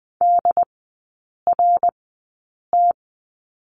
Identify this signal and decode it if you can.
Morse code, heard as a keyed tone, about 20 wpm, spaced as for 8 wpm, 690 Hz. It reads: DRT
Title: 081 Lesson 14e - Number 2 - Three Characters 20wpm